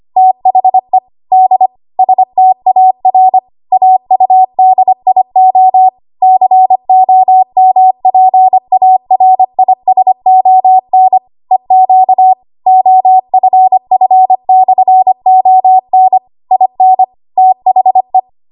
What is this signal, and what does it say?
Morse code, heard as a keyed tone, about 25 wpm, 750 Hz. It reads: THE D STAR AUDIO COMPARISON EQ OFF/ON IN THE
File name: Archive / 25 WPM / 25WPM